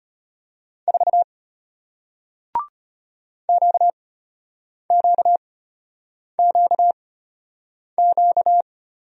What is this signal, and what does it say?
Morse code, heard as a keyed tone, about 34 wpm, spaced as for 8 wpm, 700 Hz. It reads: 4 QQQQ